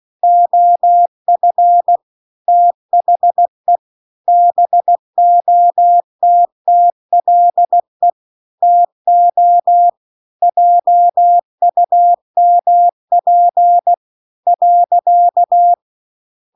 Morse code, heard as a keyed tone, about 16 wpm, 700 Hz. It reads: OF THE BOTTLE TO JUMP .